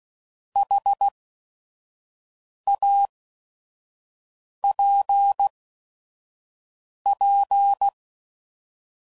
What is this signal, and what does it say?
Morse code, heard as a keyed tone, about 16 wpm, spaced as for 5 wpm, 800 Hz. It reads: HAPP
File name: Mixed / Misc / Words